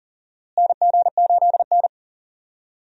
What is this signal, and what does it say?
Morse code, heard as a keyed tone, about 38 wpm, 700 Hz. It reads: NG8N
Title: USA Call Signs - 4 Characters - Repeated 2x - 40wpm